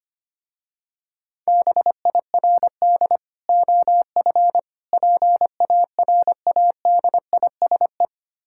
Morse code, heard as a keyed tone, about 25 wpm, 700 Hz. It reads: BIRD OF PARADISE
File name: practice / 25wpm / Difficult Words with Hints - 25wpm